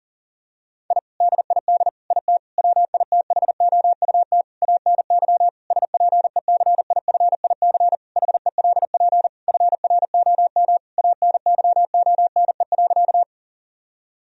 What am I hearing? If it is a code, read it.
Morse code, heard as a keyed tone, about 40 wpm, 700 Hz. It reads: I DID IT WITHOUT ANY SPECIFIC HELP FROM ANYONE.
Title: Sentences from Top 1000 Words - Review Part 3 - 40wpm